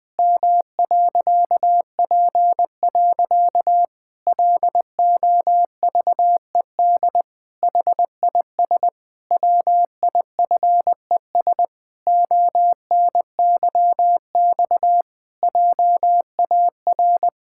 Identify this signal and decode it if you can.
Morse code, heard as a keyed tone, about 20 wpm, 700 Hz. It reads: M.P. LOVED HIS WIFES ONYX JAR
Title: Pangram Challenge v2 - 20wpm